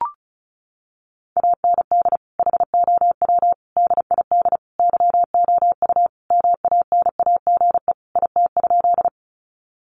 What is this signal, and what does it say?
Morse code, heard as a keyed tone, about 34 wpm, 700 Hz. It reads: AND HOW DID YOU MANAGE IT?
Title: Sentences from Top 1400 Words 35wpm